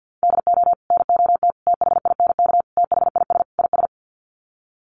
Morse code, heard as a keyed tone, about 50 wpm, 700 Hz. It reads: DO NOT THINK THIS IS
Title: Sets of 5 Words Encore2 50wpm